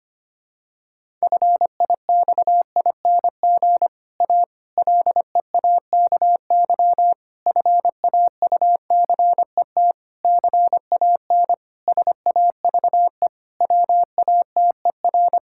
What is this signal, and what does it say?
Morse code, heard as a keyed tone, about 25 wpm, 700 Hz. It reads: FIXING A LEAKY FAUCET CAN SAVE WATER
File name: practice / 25wpm / Difficult Sentences with Hints - 25wpm